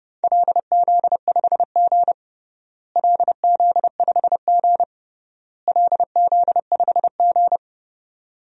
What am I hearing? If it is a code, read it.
Morse code, heard as a keyed tone, about 32 wpm, 700 Hz. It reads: LZ5G LZ5G LZ5G
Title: DXCC-100 Call Signs - 4 Characters - Repeated 3x - 30wpm